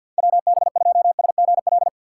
Morse code, heard as a keyed tone, about 52 wpm, 700 Hz. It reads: WB2SGL
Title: USA Call Signs - 3x Character Spacing - 50wpm